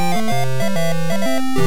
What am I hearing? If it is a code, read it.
There is no code.